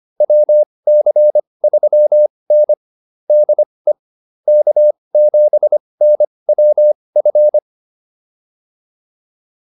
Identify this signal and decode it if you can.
Morse code, heard as a keyed tone, about 25 wpm, spaced as for 20 wpm, 600 Hz. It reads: WC3N DE K7NWF